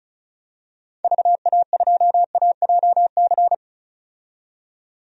Morse code, heard as a keyed tone, about 34 wpm, 700 Hz. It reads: VA2AJC
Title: DXCC-100 Call Signs - 6 Characters - 35wpm